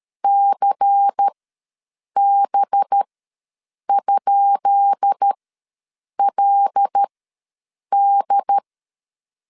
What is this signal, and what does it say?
Morse code, heard as a keyed tone, about 13 wpm, spaced as for 7 wpm, 800 Hz. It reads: CB?LD
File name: LEZIONI / 10 SIGNS / 10B INT ws 12wpm 3527